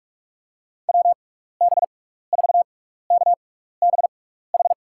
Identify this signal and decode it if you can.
Morse code, heard as a keyed tone, about 46 wpm, spaced as for 15 wpm, 700 Hz. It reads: WB4KBH
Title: USA Call Signs - 6 Characters - 6x Character Spacing - 45wpm